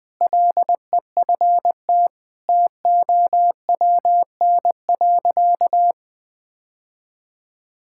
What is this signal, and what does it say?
Morse code, heard as a keyed tone, about 20 wpm, 700 Hz. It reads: LEFT TOWN.
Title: Sentences from Top 1000 Words - Review Part 6 - 20wpm